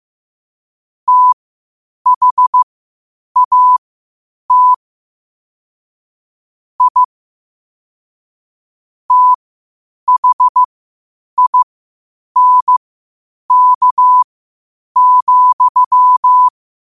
Morse code, heard as a keyed tone, about 15 wpm, spaced as for 8 wpm, 1000 Hz. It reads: THAT I THINK,
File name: qotdfiles / 8wpm